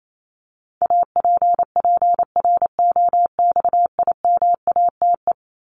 Morse code, heard as a keyed tone, about 28 wpm, 700 Hz. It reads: APPROXIMATE
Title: LY Suffix - Top 100 Words with Base Words 28wpm